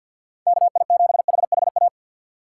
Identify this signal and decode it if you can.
Morse code, heard as a keyed tone, about 51 wpm, 700 Hz. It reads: KI7HHA
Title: USA Call Signs - 6 Characters - 4x Character Spacing - 50wpm